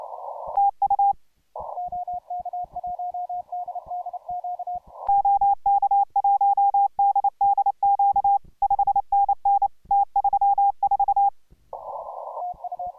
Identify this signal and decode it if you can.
Morse code, heard as a keyed tone, about 29 wpm, 800 Hz. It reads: TU OK1DDQ 5NN T34